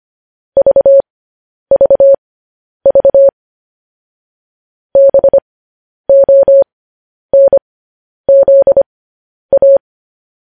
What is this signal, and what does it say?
Morse code, heard as a keyed tone, about 25 wpm, spaced as for 10 wpm, 550 Hz. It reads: VVV BONZA
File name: podcast / News headlines-1x-25x10